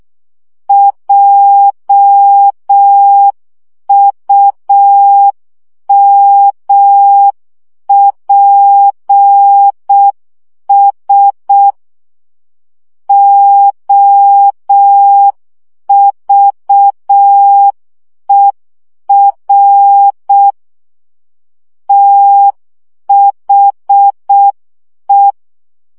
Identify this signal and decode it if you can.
Morse code, heard as a keyed tone, about 6 wpm, 800 Hz. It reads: JUMPS OVER THE
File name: CW 5 baud